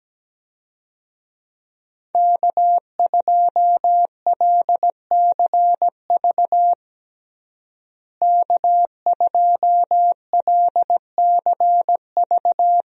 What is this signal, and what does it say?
Morse code, heard as a keyed tone, about 17 wpm, 700 Hz. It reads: K2LCV K2LCV
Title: USA Call Signs - 5 Characters - Repeated 2x - 17wpm